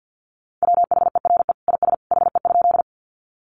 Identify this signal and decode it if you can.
Morse code, heard as a keyed tone, about 50 wpm, 700 Hz. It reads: WHERE IS HE?
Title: Sentences from Top 200 Words 50wpm